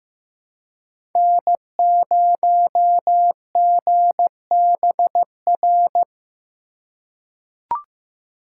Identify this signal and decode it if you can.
Morse code, heard as a keyed tone, about 15 wpm, 700 Hz. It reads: N0GBR
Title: USA Call Signs - 5 Characters - 2x Character Spacing - 15wpm